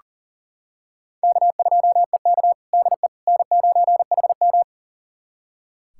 Morse code, heard as a keyed tone, about 40 wpm, 700 Hz. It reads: K2EK DE N9HM
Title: USA Callsign DE Callsign - 4 Characters - 40wpm